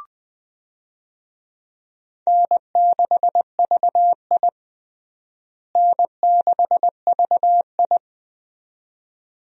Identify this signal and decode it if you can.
Morse code, heard as a keyed tone, about 20 wpm, 700 Hz. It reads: N6VI N6VI
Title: USA Call Signs - 4 Characters - Repeated 2x - 20wpm